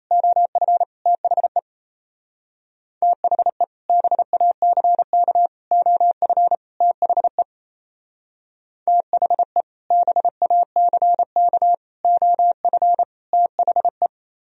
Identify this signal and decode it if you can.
Morse code, heard as a keyed tone, about 32 wpm, 700 Hz. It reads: OF THE THE BACK OF THE THE BACK OF THE
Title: Speed-Racing - Sets of 4 Words 28wpm